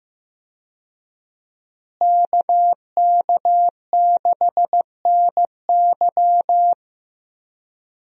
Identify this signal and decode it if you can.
Morse code, heard as a keyed tone, about 15 wpm, 700 Hz. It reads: KK6NY
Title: USA Call Signs - 5 Characters - Repeated 2x - 15wpm